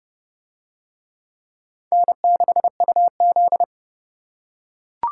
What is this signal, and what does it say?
Morse code, heard as a keyed tone, about 28 wpm, 700 Hz. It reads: N6UZ